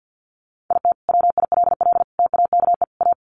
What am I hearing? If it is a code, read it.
Morse code, heard as a keyed tone, about 50 wpm, 700 Hz. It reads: IT WILL TAKE A
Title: Sentences from Top 200 Words 50wpm